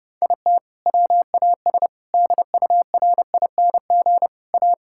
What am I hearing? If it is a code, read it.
Morse code, heard as a keyed tone, about 30 wpm, 700 Hz. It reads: IT WAS DURING A
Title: Sentences from Top 1000 Words - Review Part 5 - 30wpm